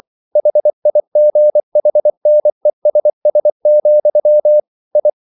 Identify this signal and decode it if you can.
Morse code, heard as a keyed tone, about 24 wpm, 600 Hz. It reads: HIGHNESS, I
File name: CW / Practice / EnglishQuotations0424